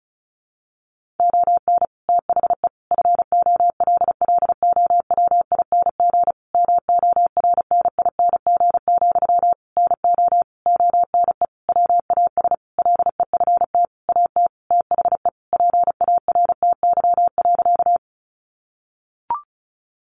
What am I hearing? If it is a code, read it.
Morse code, heard as a keyed tone, about 34 wpm, 700 Hz. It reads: ON THE FOLLOWING MORNING, NO ONE WAS LEFT AT THE PARTY.
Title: Sentences from Top 1400 Words 35wpm